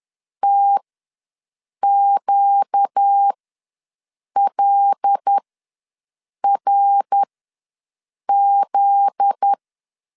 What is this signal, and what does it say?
Morse code, heard as a keyed tone, about 10 wpm, spaced as for 6 wpm, 800 Hz. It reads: TQLRZ
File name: LEZIONI / 7 GF9Z / 7 GF9Z ws 10wpm 3919